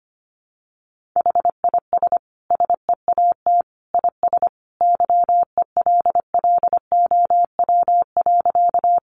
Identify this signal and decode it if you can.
Morse code, heard as a keyed tone, about 25 wpm, 700 Hz. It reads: HIS SEAT IS YELLOW.
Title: Additional Easy Phrases 25wpm